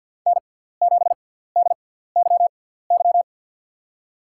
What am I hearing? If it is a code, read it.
Morse code, heard as a keyed tone, about 47 wpm, spaced as for 17 wpm, 700 Hz. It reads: N7DYY
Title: USA Call Signs - 5 Characters - 6x Character Spacing - 50wpm